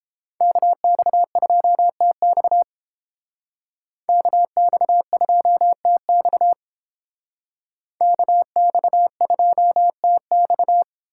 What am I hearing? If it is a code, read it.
Morse code, heard as a keyed tone, about 30 wpm, 700 Hz. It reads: KX2TX KX2TX KX2TX